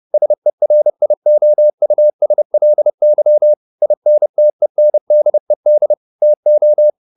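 Morse code, heard as a keyed tone, about 30 wpm, 600 Hz. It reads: SERIOUSLY INTENDED TO
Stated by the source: Morse code at 30 wpm, the novel Never by Ken Follett in English, from CW med Gnister.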